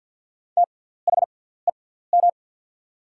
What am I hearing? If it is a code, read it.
Morse code, heard as a keyed tone, about 49 wpm, spaced as for 17 wpm, 700 Hz. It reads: THEM
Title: 6x Character Spacing - Top 100 Words - 50wpm